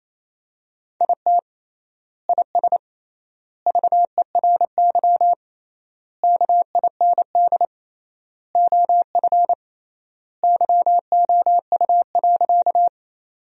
Morse code, heard as a keyed tone, about 28 wpm, 700 Hz. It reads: IT IS VERY KIND OF YOU.
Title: 3x Word-Spacing - Sentences from Top 200 Words - 28wpm